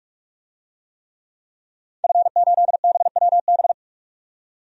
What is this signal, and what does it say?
Morse code, heard as a keyed tone, about 46 wpm, 700 Hz. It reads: F8BWB
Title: DXCC-100 Call Signs - 5 Characters - 45wpm